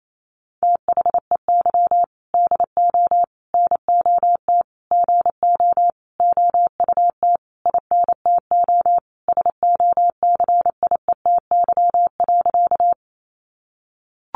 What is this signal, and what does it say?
Morse code, heard as a keyed tone, about 28 wpm, 700 Hz. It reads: THEY DO NOT GO OUT INTO SOCIETY.
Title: Sentences from Top 700 Words 28wpm